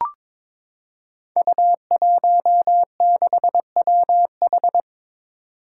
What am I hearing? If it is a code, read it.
Morse code, heard as a keyed tone, about 22 wpm, 700 Hz. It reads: U16WH
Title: Five Challenging Letter-Number Combinations 22wpm